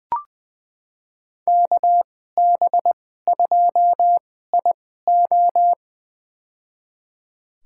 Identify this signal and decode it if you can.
Morse code, heard as a keyed tone, about 20 wpm, spaced as for 14 wpm, 700 Hz. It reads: KB2IO
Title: USA Call Signs - 5 Characters - 2x Character Spacing - 20wpm